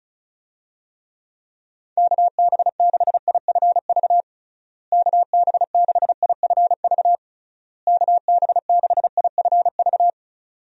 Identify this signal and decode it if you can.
Morse code, heard as a keyed tone, about 34 wpm, 700 Hz. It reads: KB6IFV KB6IFV KB6IFV
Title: USA Call Signs - 6 Characters - Repeated 3x - 35wpm